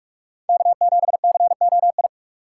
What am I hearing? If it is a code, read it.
Morse code, heard as a keyed tone, about 46 wpm, 700 Hz. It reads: K7COI
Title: USA Call Signs - 5 Characters - 2x Character Spacing - 45wpm